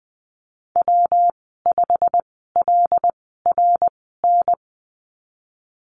Morse code, heard as a keyed tone, about 19 wpm, spaced as for 14 wpm, 700 Hz. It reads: W5LRN
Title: 147 Lesson 24g - Letter y - Call Signs 20wpm